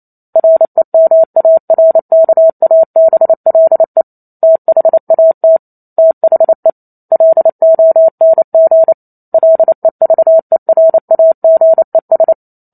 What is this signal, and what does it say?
Morse code, heard as a keyed tone, about 28 wpm, 650 Hz. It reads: REMARKABLE THAT THE LONG LEVERAGES
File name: Chapter 2 - 28 WPM